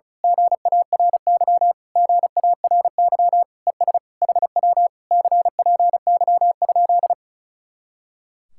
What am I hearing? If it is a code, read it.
Morse code, heard as a keyed tone, about 32 wpm, 700 Hz. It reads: GARY GARY ES HW CPY?
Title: Speed-Racing Common QSO Phrases Encore - 35wpm